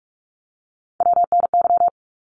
Morse code, heard as a keyed tone, about 46 wpm, 700 Hz. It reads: WNY
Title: ARRL Sections 45wpm